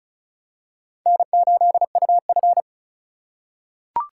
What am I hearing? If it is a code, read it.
Morse code, heard as a keyed tone, about 34 wpm, 700 Hz. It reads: N8UF
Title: USA Call Signs - 4 Characters - 4x Character Spacing - 35wpm